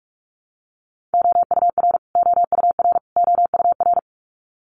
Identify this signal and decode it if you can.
Morse code, heard as a keyed tone, about 44 wpm, 700 Hz. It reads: OUR OUR OUR
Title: Top 200 Words Repeated 45wpm